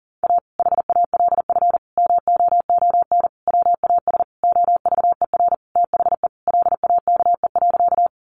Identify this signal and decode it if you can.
Morse code, heard as a keyed tone, about 37 wpm, 700 Hz. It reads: A HALF MOON WAS OVER THE LAKE.
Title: Sentences from Top 1100 Words 40wpm